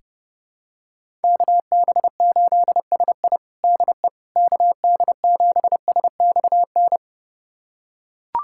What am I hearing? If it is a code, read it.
Morse code, heard as a keyed tone, about 30 wpm, 700 Hz. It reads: KB8SI DE KD7SXN
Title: USA Callsign DE Callsign 30wpm